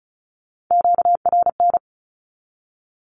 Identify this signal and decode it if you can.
Morse code, heard as a keyed tone, about 34 wpm, 700 Hz. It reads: QRN
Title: Q-Codes Repeated 3x 35wpm